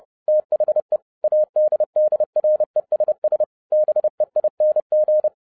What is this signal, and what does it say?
Morse code, heard as a keyed tone, about 30 wpm, 610 Hz. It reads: THE ADDRESS BEING